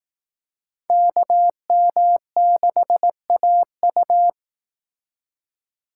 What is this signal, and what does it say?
Morse code, heard as a keyed tone, about 18 wpm, 700 Hz. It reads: KM6AU
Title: Speed Racing - USA 2x2 Call Signs 15wpm